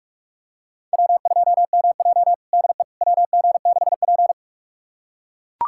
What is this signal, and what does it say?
Morse code, heard as a keyed tone, about 44 wpm, 700 Hz. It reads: W2MJ DE WG6P
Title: USA Callsign DE Callsign - 4 Characters - 45wpm